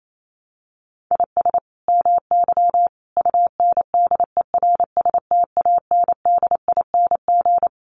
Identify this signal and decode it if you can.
Morse code, heard as a keyed tone, about 28 wpm, 700 Hz. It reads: IS MY UNDERSTANDING
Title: Sentences from Top 1400 Words 28wpm